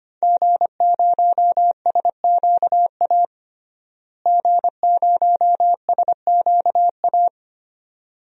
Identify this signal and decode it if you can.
Morse code, heard as a keyed tone, about 25 wpm, 700 Hz. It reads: G0SQA G0SQA
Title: DXCC-100 Call Signs - 5 Characters - Repeated 2x - 25wpm